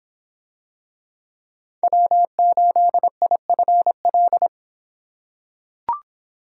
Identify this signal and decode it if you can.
Morse code, heard as a keyed tone, about 26 wpm, 700 Hz. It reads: W8IFL